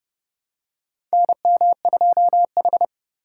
Morse code, heard as a keyed tone, about 30 wpm, 700 Hz. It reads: NM2H
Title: USA Call Signs - 4 Characters - 4x Character Spacing - 30wpm